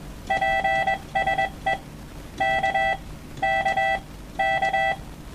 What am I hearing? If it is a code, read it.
Morse code, heard as a keyed tone, about 21 wpm, 780 Hz, with noise. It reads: PSE K K K
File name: morsen-Dateien / cq de df2jb